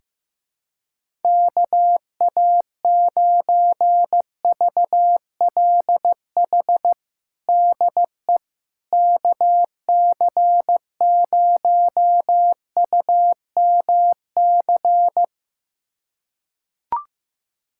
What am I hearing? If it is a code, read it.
Morse code, heard as a keyed tone, about 15 wpm, 700 Hz. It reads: KA9VLH DE KC0UMC
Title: USA Callsign DE Callsign - 6 Characters - 15wpm